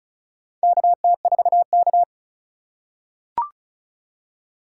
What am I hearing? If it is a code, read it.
Morse code, heard as a keyed tone, about 34 wpm, 700 Hz. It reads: KT4K
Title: USA Call Signs - 4 Characters - 3x Character Spacing - 35wpm